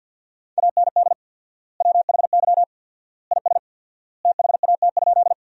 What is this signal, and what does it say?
Morse code, heard as a keyed tone, about 50 wpm, 700 Hz. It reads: AND WHY IS THAT?
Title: 4x Word-Spacing - Sentences from Top 200 Words - 50wpm